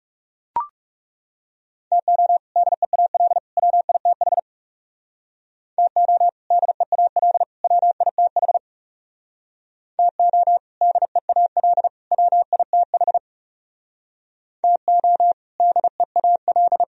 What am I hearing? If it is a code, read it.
Morse code, heard as a keyed tone, about 37 wpm, 700 Hz. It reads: TO DEAL WITH TO DEAL WITH TO DEAL WITH TO DEAL
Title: Speed-Racing - Sets of 3 Words 30wpm